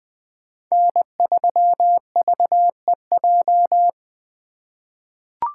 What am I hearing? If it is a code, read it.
Morse code, heard as a keyed tone, about 21 wpm, 700 Hz. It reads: N3VEJ
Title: 2023 POTA Activator Call Signs - 20wpm